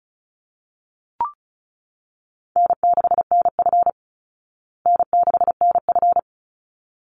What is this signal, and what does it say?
Morse code, heard as a keyed tone, about 34 wpm, 700 Hz. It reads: N6NF N6NF
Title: Easy Call Signs Repeated 2x - 35wpm